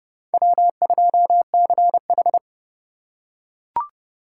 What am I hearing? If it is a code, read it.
Morse code, heard as a keyed tone, about 28 wpm, 700 Hz. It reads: W2CH